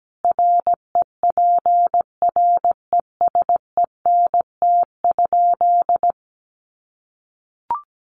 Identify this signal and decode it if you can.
Morse code, heard as a keyed tone, about 17 wpm, 700 Hz. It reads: REPRESENT?
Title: Sentences from Top 1200 Words 17wpm